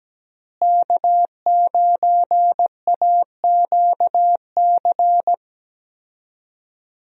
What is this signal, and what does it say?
Morse code, heard as a keyed tone, about 17 wpm, 700 Hz. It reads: K9AQC
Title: USA Call Signs - 5 Characters - Repeated 2x - 17wpm